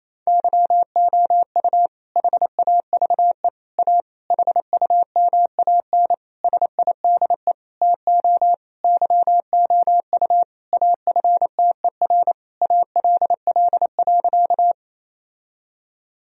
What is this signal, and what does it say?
Morse code, heard as a keyed tone, about 28 wpm, 690 Hz. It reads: YOU HAVE A HUMAN SIDE TO YOU AFTER ALL.
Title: Sentences from Top 1000 Words - Review Part 3 - 28wpm